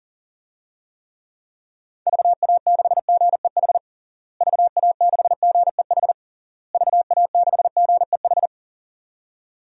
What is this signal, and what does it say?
Morse code, heard as a keyed tone, about 40 wpm, 700 Hz. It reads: VA6GEH VA6GEH VA6GEH
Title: DXCC-100 Call Signs - 6 Characters - Repeated 3x - 40wpm